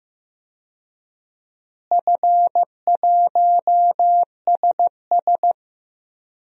Morse code, heard as a keyed tone, about 15 wpm, 700 Hz. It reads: F1SS
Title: DXCC-100 Call Signs - 4 Characters - Repeated 2x - 15wpm